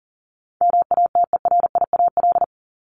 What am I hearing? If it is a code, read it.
Morse code, heard as a keyed tone, about 40 wpm, 700 Hz. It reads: MATERIAL
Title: ES:S Suffix - Top 100 Words with Base Words 40wpm